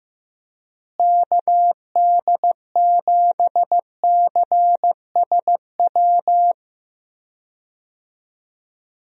Morse code, heard as a keyed tone, about 15 wpm, 700 Hz. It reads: KD7CSW
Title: USA Call Signs - 6 Characters - Repeated 3x - 15wpm